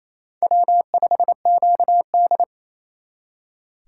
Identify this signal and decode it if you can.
Morse code, heard as a keyed tone, about 32 wpm, 700 Hz. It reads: W5QD
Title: Speed Racing - USA Call Signs 28wpm